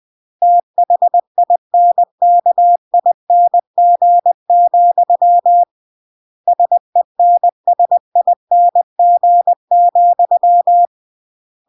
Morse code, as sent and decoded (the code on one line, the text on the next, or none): - .... .. -. -.- .. -. --. --..-- / ... -. ... .. -. --. --..--
THINKING, SENSING,